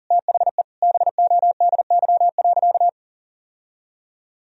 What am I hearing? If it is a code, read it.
Morse code, heard as a keyed tone, about 40 wpm, 700 Hz. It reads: THE BODY.
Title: Sentences from Top 1000 Words - Review Part 4 - 40wpm